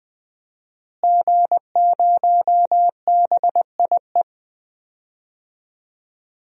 Morse code, heard as a keyed tone, about 21 wpm, 700 Hz. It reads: G0BIE
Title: DXCC-100 Call Signs - 5 Characters - Repeated 3x - 20wpm